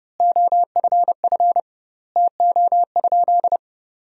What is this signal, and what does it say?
Morse code, heard as a keyed tone, about 30 wpm, 700 Hz. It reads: OFF TO?
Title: 2x Word-Spacing - Sentences from Top 300 Words - 30wpm